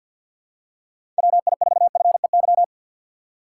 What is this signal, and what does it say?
Morse code, heard as a keyed tone, about 52 wpm, 700 Hz. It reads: WI4FEY
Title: USA Call Signs - 4x Character Spacing - 50wpm